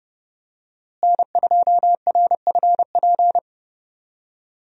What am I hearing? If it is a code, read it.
Morse code, heard as a keyed tone, about 30 wpm, 700 Hz. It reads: N2RFP
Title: USA Call Signs - 5 Characters - Repeated 3x - 30wpm